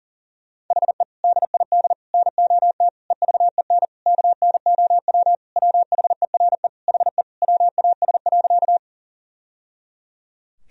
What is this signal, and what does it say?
Morse code, heard as a keyed tone, about 40 wpm, 700 Hz. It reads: HE DID NOT EVEN KNOW WHERE HE WAS.
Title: Sentences from Top 1000 Words - Review Part 1 - 40wpm